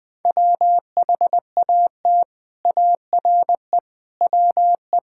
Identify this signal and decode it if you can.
Morse code, heard as a keyed tone, about 20 wpm, 700 Hz. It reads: WHAT ARE WE
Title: Sentences with 8-Word Phrases - Spoken Beginning - 20wpm